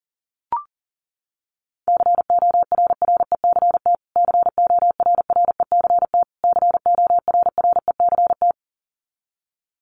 Morse code, heard as a keyed tone, about 37 wpm, 700 Hz. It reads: CORRECT CORRECT CORRECT
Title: Top 500 Words Repeated 40wpm